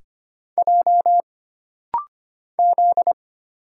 Morse code, heard as a keyed tone, about 25 wpm, 700 Hz. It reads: J Z
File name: Single Character (Letter:Number:Punctuation:Prosign) - Rapid-Fire - 25wpm